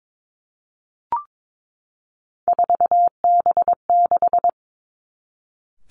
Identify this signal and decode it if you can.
Morse code, heard as a keyed tone, about 22 wpm, 700 Hz. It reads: 4B6
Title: Three Challenging Letter-Number Combinations 22wpm